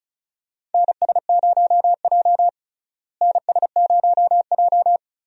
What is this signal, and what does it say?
Morse code, heard as a keyed tone, about 36 wpm, 700 Hz. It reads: NS0J NS0J
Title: USA Call Signs - 4 Characters - Repeated 2x - 35wpm